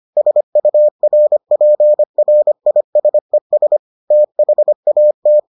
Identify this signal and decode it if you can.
Morse code, heard as a keyed tone, about 25 wpm, 600 Hz. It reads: SURPRISES THAT